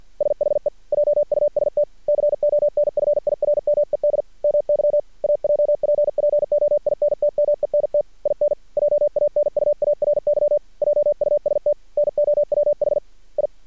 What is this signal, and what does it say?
Morse code, heard as a keyed tone, about 50 wpm, 590 Hz. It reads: SHE JUST CONFIRMED MY AJPPOINTMENT IN JANUARY JUST NOWH I
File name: archived / CRSnet